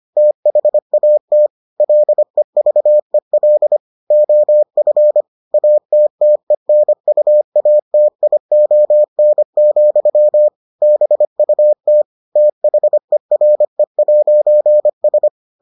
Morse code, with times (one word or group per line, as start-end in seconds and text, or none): 0.2-1.5 THAT
1.8-3.8 LEVEL
4.1-5.2 OF
5.5-10.5 ATTENUATION,
10.8-12.0 BUT
12.4-15.3 THERE'S